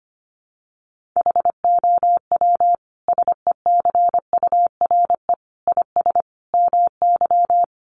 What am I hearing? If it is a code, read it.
Morse code, heard as a keyed tone, about 25 wpm, 700 Hz. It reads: HOW SECURE IS MY
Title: Sentences from Top 1200 Words 25wpm